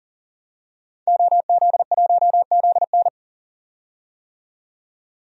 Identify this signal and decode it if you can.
Morse code, heard as a keyed tone, about 40 wpm, 700 Hz. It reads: OZ1ZN
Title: DXCC-100 Call Signs - 5 Characters - 40wpm